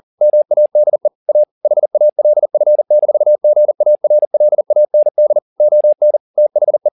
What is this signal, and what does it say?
Morse code, heard as a keyed tone, about 40 wpm, 600 Hz. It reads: MADE A HALF-GARLAND ON THE